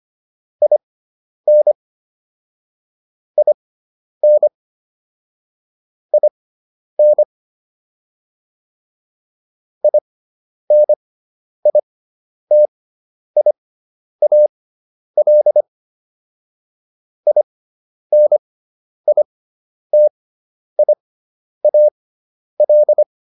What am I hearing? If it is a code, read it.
Morse code, heard as a keyed tone, about 25 wpm, spaced as for 10 wpm, 600 Hz. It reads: IN IN IN INITIAL INITIAL